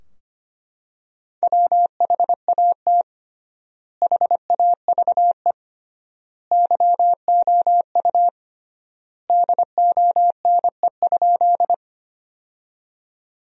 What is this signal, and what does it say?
Morse code, heard as a keyed tone, about 25 wpm, 700 Hz. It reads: WHAT HAVE YOU DONE?
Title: 3x Word-Spacing - Sentences from Top 500 Words - 25wpm